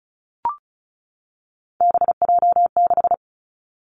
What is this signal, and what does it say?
Morse code, heard as a keyed tone, about 36 wpm, 700 Hz. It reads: BJ6
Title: Three Challenging Letter-Number Combinations 35wpm